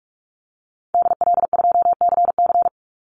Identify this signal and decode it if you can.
Morse code, heard as a keyed tone, about 44 wpm, 700 Hz. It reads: DL2CC
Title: Call Signs - 45wpm